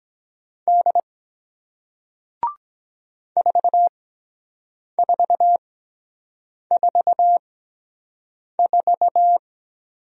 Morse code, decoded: D 4444

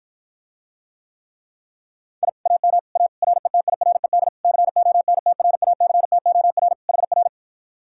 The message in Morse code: I AM A RETIRED CONTRACTOR HR

